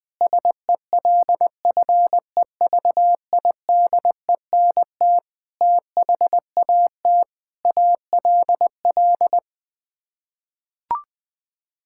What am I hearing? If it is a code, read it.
Morse code, heard as a keyed tone, about 21 wpm, 700 Hz. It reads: SELFEVIDENT THAT ALL
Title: Sets of 9 Words - 20wpm